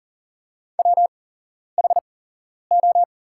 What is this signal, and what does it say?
Morse code, heard as a keyed tone, about 40 wpm, spaced as for 11 wpm, 700 Hz. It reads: WHO
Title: 8x Character Spacing - Top 200 Words - 40wpm